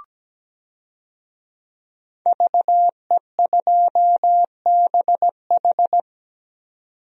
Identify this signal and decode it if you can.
Morse code, heard as a keyed tone, about 17 wpm, 700 Hz. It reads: VE2BH
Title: DXCC-100 Call Signs - 5 Characters - 17wpm